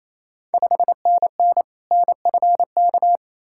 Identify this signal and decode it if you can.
Morse code, heard as a keyed tone, about 28 wpm, 690 Hz. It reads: 5NN NFK
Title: Ontario QSO Party Exchange 28wpm